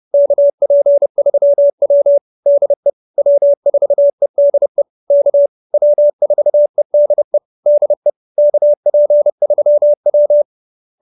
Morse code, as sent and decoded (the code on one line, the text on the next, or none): -.- .--. ...-- .-- / -.. / .-- ....- . -.. / -.- / .-- ....- . -.. / -.. / -.- .--. ...-- .--
KP3W DE W4EDE K W4EDE DE KP3W